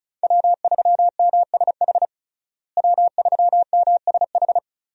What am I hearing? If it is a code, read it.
Morse code, heard as a keyed tone, about 36 wpm, 700 Hz. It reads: W3MSH W3MSH
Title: USA Call Signs - 5 Characters - Repeated 3x - 35wpm